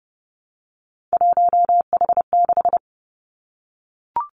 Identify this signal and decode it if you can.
Morse code, heard as a keyed tone, about 30 wpm, 700 Hz. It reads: 1H6
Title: Three Challenging Letter-Number Combinations 30wpm